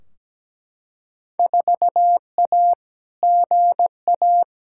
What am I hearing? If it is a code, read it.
Morse code, heard as a keyed tone, about 17 wpm, 700 Hz. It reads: 4A GA
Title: Field Day Exchanges - 17wpm